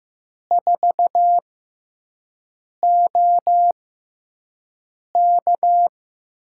Morse code, heard as a keyed tone, about 15 wpm, spaced as for 5 wpm, 700 Hz. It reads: 4OK
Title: USA Call Signs - 5 Characters - 6x Character Spacing - 15wpm